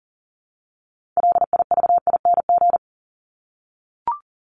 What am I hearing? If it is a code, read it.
Morse code, heard as a keyed tone, about 40 wpm, 700 Hz. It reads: LIVING
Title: ING Suffix - Top 100 Words with Base Words 40wpm